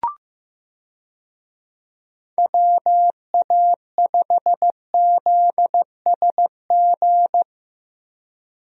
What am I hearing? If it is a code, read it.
Morse code, heard as a keyed tone, about 16 wpm, 700 Hz. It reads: WA5ZSG
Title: USA Call Signs - 6 Characters - 15wpm